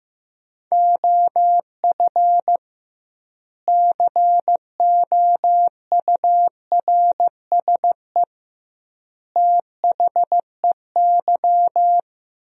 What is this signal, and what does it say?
Morse code, heard as a keyed tone, about 15 wpm, 700 Hz. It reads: OF COURSE THEY
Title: 2x Word-Spacing - Sentences from Top 500 Words - 15wpm